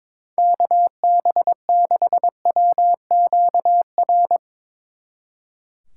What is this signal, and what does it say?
Morse code, heard as a keyed tone, about 24 wpm, 700 Hz. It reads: KB6WQR
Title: Speed Racing - USA Call Signs 22wpm